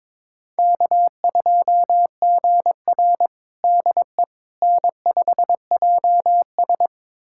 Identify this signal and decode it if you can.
Morse code, heard as a keyed tone, about 22 wpm, 700 Hz. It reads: K2GR DE N5JS
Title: USA Callsign DE Callsign - 4 Characters - 22wpm